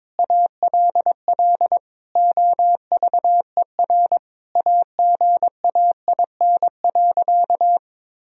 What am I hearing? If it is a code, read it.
Morse code, heard as a keyed tone, about 22 wpm, 700 Hz. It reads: ALL OVER AGAIN.
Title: Sentences from Top 1000 Words - Review Part 1 - 22wpm